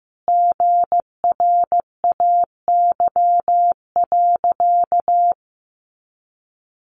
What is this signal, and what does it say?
Morse code, heard as a keyed tone, about 15 wpm, 700 Hz. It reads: GRAY.
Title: Additional Easy Phrases 15wpm